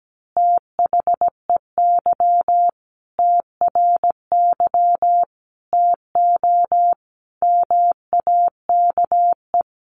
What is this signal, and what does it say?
Morse code, heard as a keyed tone, about 17 wpm, 700 Hz. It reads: THEY TRY TO MAKE
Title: Easy Phrases 17wpm